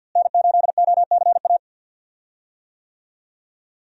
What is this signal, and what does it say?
Morse code, heard as a keyed tone, about 51 wpm, 700 Hz. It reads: N8OCA
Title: USA Call Signs - 5 Characters - Repeated 3x - 50wpm